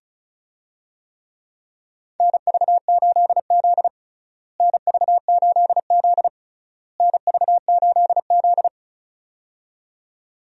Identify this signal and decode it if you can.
Morse code, heard as a keyed tone, about 36 wpm, 700 Hz. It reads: NV8Z NV8Z NV8Z
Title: USA Call Signs - 4 Characters - Repeated 3x - 35wpm